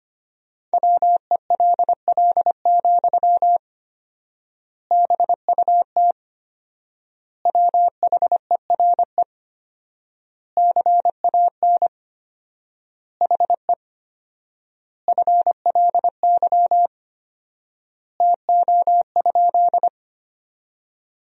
Morse code, heard as a keyed tone, about 25 wpm, 700 Hz. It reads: WELL, BUT WHERE CAN HE FLY TO?
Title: 4x Word-Spacing - Sentences from Top 500 Words - 25wpm